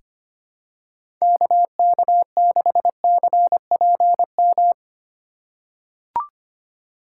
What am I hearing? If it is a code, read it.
Morse code, heard as a keyed tone, about 25 wpm, 700 Hz. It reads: KK6CPM
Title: USA Call Signs - 6 Characters - 2x Character Spacing - 25wpm